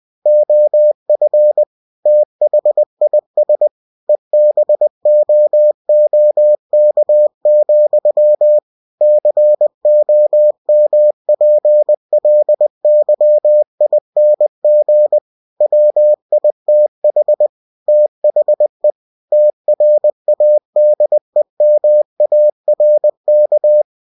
Morse code, heard as a keyed tone, about 20 wpm, 600 Hz. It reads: OF THIS EBOOK, COMPLYING WITH THE TRADEMARK
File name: cw-books / winnie-the-pooh / winnie-20wpm-page-0112